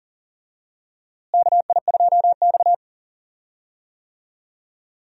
Morse code, heard as a keyed tone, about 40 wpm, 700 Hz. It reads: KI2X
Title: USA Call Signs - 4 Characters - Repeated 3x - 40wpm